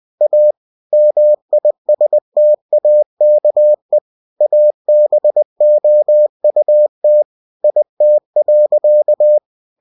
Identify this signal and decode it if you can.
Morse code, heard as a keyed tone, about 20 wpm, 600 Hz. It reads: A MISTAKE ABOUT IT.
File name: cw-books / winnie-the-pooh / winnie-20wpm-page-0027